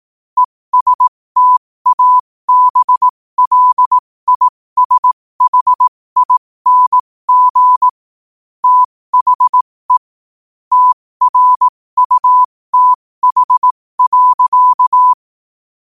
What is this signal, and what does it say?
Morse code, heard as a keyed tone, about 18 wpm, spaced as for 15 wpm, 1000 Hz. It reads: ESTABLISHING THE TRUTH.
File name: qotdfiles / 15wpm